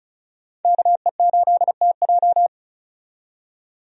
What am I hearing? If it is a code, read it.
Morse code, heard as a keyed tone, about 36 wpm, 700 Hz. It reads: KE8TJ
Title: USA Call Signs - 5 Characters - 35wpm